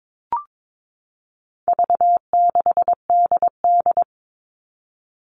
Morse code, heard as a keyed tone, about 22 wpm, 700 Hz. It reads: V6DD
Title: All Cut Numbers - 4-Digit 22wpm